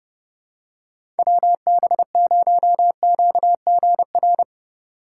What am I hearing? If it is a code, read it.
Morse code, heard as a keyed tone, about 30 wpm, 700 Hz. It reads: WB0QGR
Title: USA Call Signs - 3x Character Spacing - 30wpm